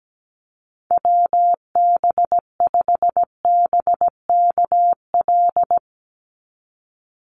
Morse code, heard as a keyed tone, about 17 wpm, 700 Hz. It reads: WB5BKL
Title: NA SOTA Chaser Callsigns 17wpm